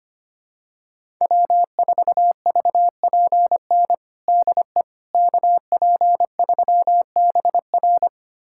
Morse code, heard as a keyed tone, about 25 wpm, 700 Hz. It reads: W4VPN DE KP3BR
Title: USA Callsign DE Callsign - 5 Characters - 25wpm